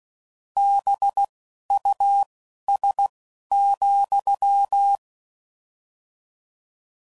Morse code, heard as a keyed tone, about 16 wpm, spaced as for 11 wpm, 800 Hz. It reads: BUS,